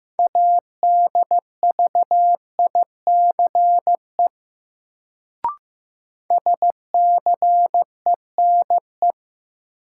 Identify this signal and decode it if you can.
Morse code, heard as a keyed tone, about 15 wpm, 700 Hz. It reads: ADVICE SCENE
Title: Sentences from Top 1100 Words - New Word Drill Alt - 15wpm